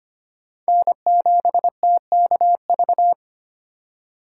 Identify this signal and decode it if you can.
Morse code, heard as a keyed tone, about 25 wpm, 700 Hz. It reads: N7TKV
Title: USA Call Signs - 5 Characters - 2x Character Spacing - 25wpm